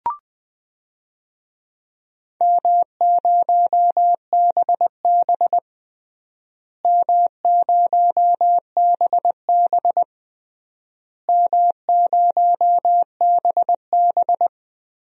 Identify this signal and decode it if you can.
Morse code, heard as a keyed tone, about 20 wpm, 700 Hz. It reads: M0BB M0BB M0BB
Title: DXCC-100 Call Signs - 4 Characters - Repeated 3x - 20wpm